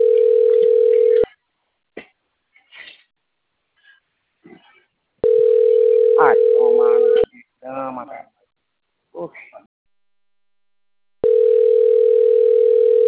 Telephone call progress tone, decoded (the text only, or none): ringback tone